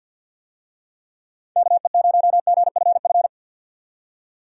Morse code, heard as a keyed tone, about 50 wpm, 700 Hz. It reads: KE0GFF
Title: USA Call Signs - 6 Characters - 50wpm